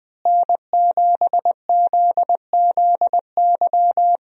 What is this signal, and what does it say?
Morse code, heard as a keyed tone, about 21 wpm, 700 Hz. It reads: N7ZZY
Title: USA Call Signs - 5 Characters - Repeated 3x - 20wpm